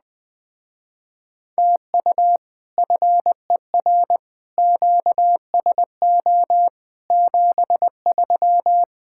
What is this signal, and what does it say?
Morse code, heard as a keyed tone, about 20 wpm, 700 Hz. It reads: TU FER QSO 73